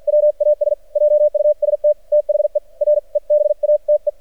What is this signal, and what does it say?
Morse code, heard as a keyed tone, about 50 wpm, spaced as for 43 wpm, 600 Hz. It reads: WAS JUST THE REBATE
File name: archived / CRSnet